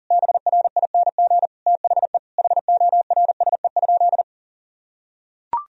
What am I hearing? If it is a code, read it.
Morse code, heard as a keyed tone, about 40 wpm, 700 Hz. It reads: BRING THE HORSE?
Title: Sentences from Top 500 Words and Abbreviations 40wpm